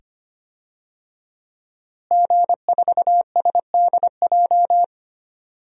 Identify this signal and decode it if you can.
Morse code, heard as a keyed tone, about 25 wpm, 700 Hz. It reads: G4SDJ
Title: DXCC-100 Call Signs - 5 Characters - Repeated 2x - 25wpm